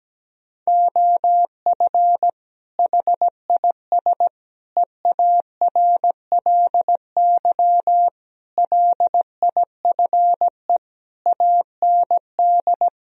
Morse code, heard as a keyed tone, about 17 wpm, 700 Hz. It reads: OF HIS EARLY LIFE AND
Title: Sentences from Top 1000 Words - Review Part 4 - 17wpm